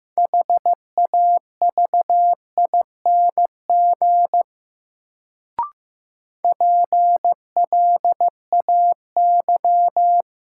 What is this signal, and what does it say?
Morse code, heard as a keyed tone, about 15 wpm, 700 Hz. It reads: HAVING PLAY
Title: Sets of 5 Words Encore - New Word Drill - 15wpm